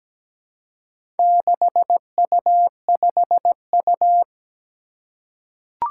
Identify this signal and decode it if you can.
Morse code, heard as a keyed tone, about 17 wpm, 700 Hz. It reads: 6U5U
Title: Four Challenging Letter-Number Combinations 17wpm